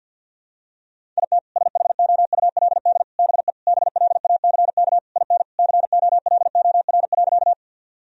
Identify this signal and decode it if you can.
Morse code, heard as a keyed tone, about 52 wpm, 700 Hz. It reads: IT SHOULD BE BLACK IN COLOR.
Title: Sentences from Top 1000 Words - Review Part 2 - 50wpm